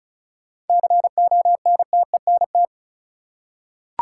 Morse code, heard as a keyed tone, about 34 wpm, 700 Hz. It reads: CONTENT
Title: Seven Letter Words - 35wpm